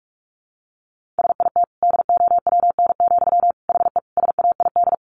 Morse code, heard as a keyed tone, about 44 wpm, 700 Hz. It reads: SIT DOWN, HE SAID
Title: Sentences from Top 1500 Words 45wpm